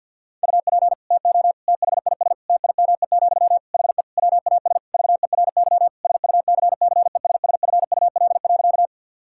Morse code, heard as a keyed tone, about 52 wpm, 700 Hz. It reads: UP TO THIS TIME, HE WAS VERY SUCCESSFUL.
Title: Sentences from Top 1000 Words - Review Part 6 - 50wpm